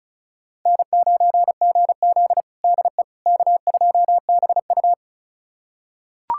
Morse code, heard as a keyed tone, about 36 wpm, 700 Hz. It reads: N9GZ DE K2BU
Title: USA Callsign DE Callsign - 4 Characters - 35wpm